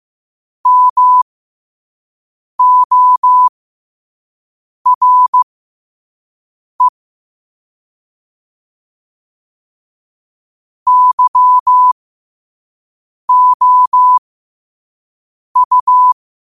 Morse code, heard as a keyed tone, about 15 wpm, spaced as for 5 wpm, 1000 Hz. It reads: MORE YOU
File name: qotdfiles / 5wpm